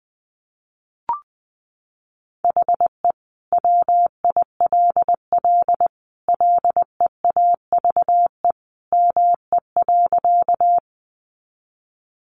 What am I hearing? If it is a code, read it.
Morse code, heard as a keyed tone, about 20 wpm, 700 Hz. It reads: HE WILL LEAVE ME.
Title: Sentences from Top 400 Words 20wpm